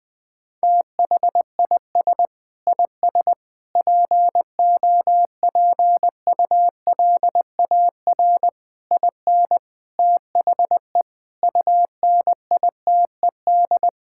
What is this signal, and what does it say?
Morse code, heard as a keyed tone, about 19 wpm, 700 Hz. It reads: THIS IS POPULAR IN THE UNITED
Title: Sentences with 5-Word Phrases - 20wpm